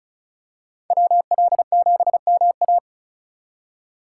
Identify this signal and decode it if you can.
Morse code, heard as a keyed tone, about 36 wpm, 700 Hz. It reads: WL7MA
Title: USA Call Signs - 3x Character Spacing - 35wpm